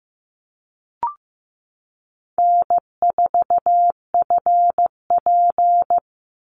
Morse code, heard as a keyed tone, about 15 wpm, 700 Hz. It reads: N4FP
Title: Call Signs Repeated 2x - 15wpm